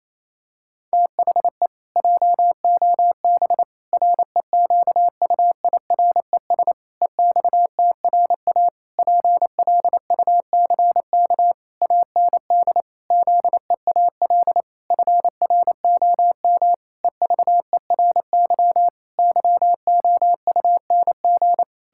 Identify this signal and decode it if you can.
Morse code, heard as a keyed tone, about 28 wpm, 700 Hz. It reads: THE JOB REQUIRES EXTRA PLUCK AND ZEAL FROM EVERY YOUNG
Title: Pangram Challenge v2 - 28wpm